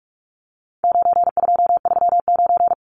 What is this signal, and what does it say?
Morse code, heard as a keyed tone, about 44 wpm, 700 Hz. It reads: 9239
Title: Numbers - 4-Digit 45wpm